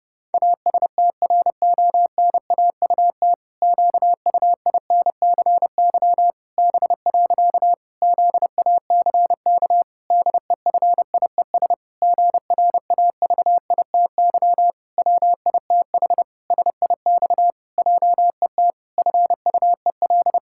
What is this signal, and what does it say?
Morse code, heard as a keyed tone, about 30 wpm, 700 Hz. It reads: ASTRONAUT QUINCY B. ZACK DEFIES GRAVITY WITH SIX JET FUEL
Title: Pangram Challenge v2 - 30wpm